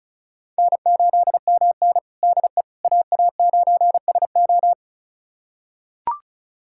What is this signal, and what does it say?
Morse code, heard as a keyed tone, about 36 wpm, 700 Hz. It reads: N8MN DE AA9SO
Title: USA Callsign DE Callsign 35wpm